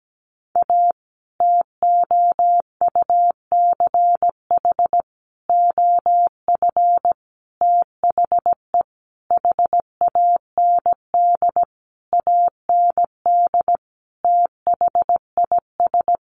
Morse code, heard as a keyed tone, about 17 wpm, 700 Hz. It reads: A TOUCH OF THE HAND AND THIS
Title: A Sound Of Thunder 17wpm